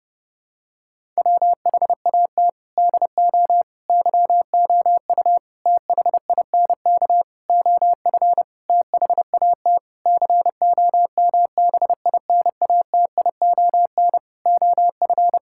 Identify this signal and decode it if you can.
Morse code, heard as a keyed tone, about 30 wpm, 700 Hz. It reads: WHAT DO YOU THINK OF THAT COMBINATION OF
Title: Sentences from Top 1000 Words - Review Part 6 - 30wpm